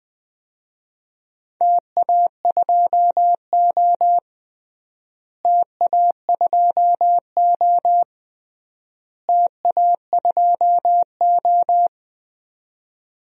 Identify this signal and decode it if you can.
Morse code, heard as a keyed tone, about 21 wpm, 700 Hz. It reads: TA2O TA2O TA2O
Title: DXCC-100 Call Signs - 4 Characters - Repeated 3x - 20wpm